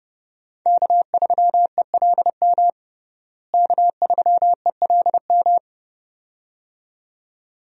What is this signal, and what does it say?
Morse code, heard as a keyed tone, about 30 wpm, 700 Hz. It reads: K3ELM K3ELM
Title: USA Call Signs - 5 Characters - Repeated 2x - 30wpm